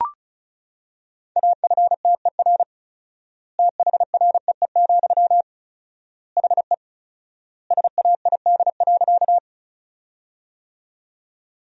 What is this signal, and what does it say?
Morse code, heard as a keyed tone, about 34 wpm, 700 Hz. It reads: AFTER THREE, HE SAID.
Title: 4x Word-Spacing - Sentences from Top 200 Words - 35wpm